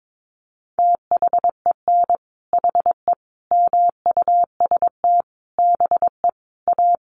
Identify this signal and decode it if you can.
Morse code, heard as a keyed tone, about 22 wpm, 700 Hz. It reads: THEN HE MUST BE A
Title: Sentences from Top 500 Words 22wpm